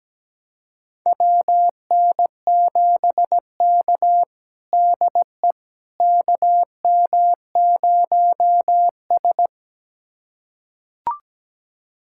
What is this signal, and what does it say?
Morse code, heard as a keyed tone, about 17 wpm, 700 Hz. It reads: WN7K DE KM0S
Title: USA Callsign DE Callsign - 4 Characters - 17wpm